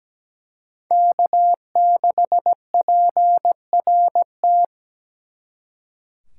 Morse code, heard as a keyed tone, about 20 wpm, 700 Hz. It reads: K6PRT
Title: Speed Racing - USA 1x3 Call Signs 17wpm